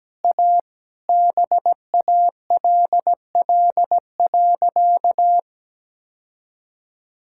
Morse code, heard as a keyed tone, about 17 wpm, 700 Hz. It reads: A BALL.
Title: Sentences from Top 1000 Words - Review Part 6 - 17wpm